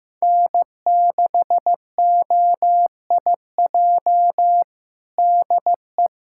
Morse code, heard as a keyed tone, about 15 wpm, 700 Hz. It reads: N6OIJ DE
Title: USA Callsign DE Callsign - 5 Characters - 15wpm